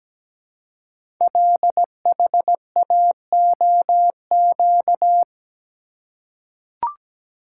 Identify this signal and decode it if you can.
Morse code, heard as a keyed tone, about 17 wpm, 700 Hz. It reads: LHAOQ
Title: Code Groups - 5 Letters - 17wpm